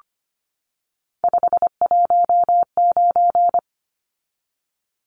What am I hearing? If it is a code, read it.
Morse code, heard as a keyed tone, about 25 wpm, 700 Hz. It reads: 519
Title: Numbers - 3-Digit 25wpm